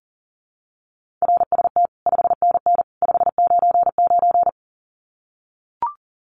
Morse code, heard as a keyed tone, about 40 wpm, 700 Hz. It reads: RST 5NN 599
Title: Common QSO Phrases 40wpm